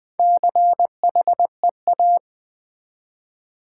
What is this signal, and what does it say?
Morse code, heard as a keyed tone, about 20 wpm, 700 Hz. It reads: CHEA
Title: Build A Word - 1 Letter At A Time - 5-Letter Words - 20wpm